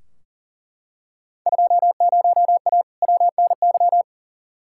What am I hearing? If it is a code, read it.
Morse code, heard as a keyed tone, about 40 wpm, 700 Hz. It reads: 20A WNY
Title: Field Day Exchanges Encore - 40wpm